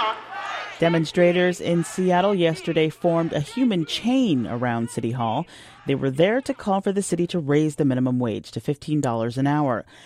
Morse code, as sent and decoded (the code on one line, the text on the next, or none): none